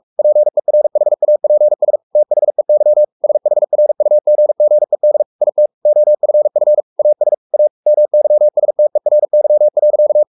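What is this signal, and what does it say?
Morse code, heard as a keyed tone, about 44 wpm, 600 Hz. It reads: PERHAPS THEY SHRUGGED IT OFF AS A MYSTERY.
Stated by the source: Morse code at 43 wpm sending Ken Follett's novel Never in English, from CW med Gnister.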